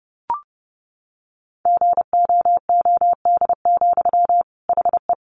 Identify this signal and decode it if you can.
Morse code, heard as a keyed tone, about 30 wpm, 700 Hz. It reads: GOOD, HE
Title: Sentences from Top 200 Words 30wpm